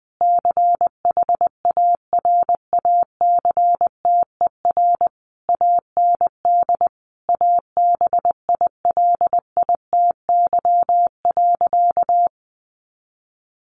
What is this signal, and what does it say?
Morse code, heard as a keyed tone, about 20 wpm, 700 Hz. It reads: CHARACTER AND ABILITY.
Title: Sentences from Top 1200 Words 20wpm